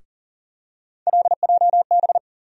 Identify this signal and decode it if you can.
Morse code, heard as a keyed tone, about 40 wpm, 700 Hz. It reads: LJB
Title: USA Call Sign Postfixes - 3 Letters 40wpm